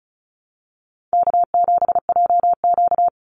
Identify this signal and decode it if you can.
Morse code, heard as a keyed tone, about 34 wpm, 700 Hz. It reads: K7JQ
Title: Call Signs Repeated 3x - 35wpm